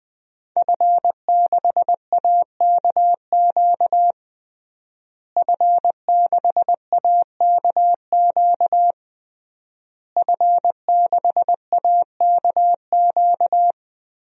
Morse code, heard as a keyed tone, about 20 wpm, 700 Hz. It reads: F6AKQ F6AKQ F6AKQ
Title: DXCC-100 Call Signs - 5 Characters - Repeated 3x - 20wpm